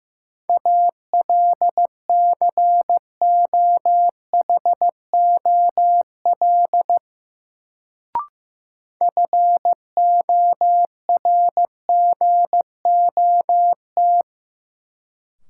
Morse code, heard as a keyed tone, about 15 wpm, 700 Hz. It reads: ALCOHOL FORGOT
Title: Sentences from Top 1400 Words - New Word Drill - 15wpm